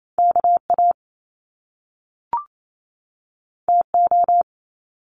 Morse code, heard as a keyed tone, about 28 wpm, 700 Hz. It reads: KA TO
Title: Top Two Letter Combinations in Words 28wpm